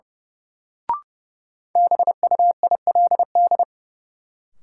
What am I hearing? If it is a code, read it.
Morse code, heard as a keyed tone, about 32 wpm, 700 Hz. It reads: BUILD
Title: Sentences from Top 1100 Words - New Word Drill Alt - 30wpm